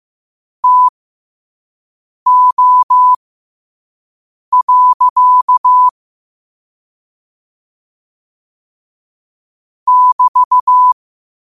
Morse code, heard as a keyed tone, about 15 wpm, spaced as for 5 wpm, 1000 Hz. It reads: TO. <BT>
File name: qotdfiles / 5wpm